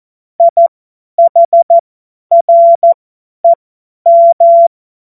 Morse code, heard as a keyed tone, about 14 wpm, spaced as for 10 wpm, 680 Hz. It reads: IHREM
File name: Morsekurs / Witze / Witz 18. Juli 2023 50 F70